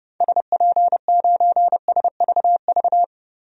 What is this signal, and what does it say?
Morse code, heard as a keyed tone, about 30 wpm, 700 Hz. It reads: SP9SVV
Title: DXCC-100 Call Signs - 6 Characters - 30wpm